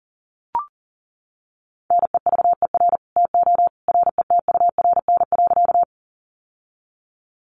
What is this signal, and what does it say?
Morse code, heard as a keyed tone, about 40 wpm, 700 Hz. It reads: NEVER TO RETURN.
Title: Sentences from Top 800 Words 40wpm